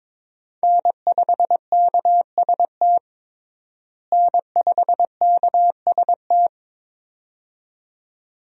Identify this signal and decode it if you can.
Morse code, heard as a keyed tone, about 22 wpm, 700 Hz. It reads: N5KST N5KST
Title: USA Call Signs - 5 Characters - Repeated 2x - 22wpm